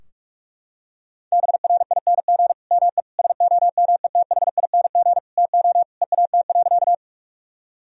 Morse code, heard as a keyed tone, about 43 wpm, 700 Hz. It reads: BRING ME SOMETHING TO EAT.
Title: Sentences from Top 1000 Words - Review Part 3 - 45wpm